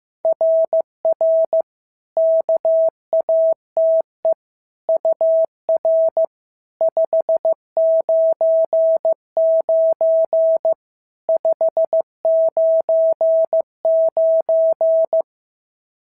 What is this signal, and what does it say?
Morse code, heard as a keyed tone, about 14 wpm, 650 Hz. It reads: RR KATE UR 599 599
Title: SOTA QSO Practice 15wpm